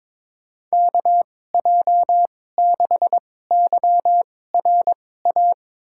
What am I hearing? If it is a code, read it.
Morse code, heard as a keyed tone, about 22 wpm, spaced as for 16 wpm, 700 Hz. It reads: KJ6YRA
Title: USA Call Signs - 6 Characters - 2x Character Spacing - 22wpm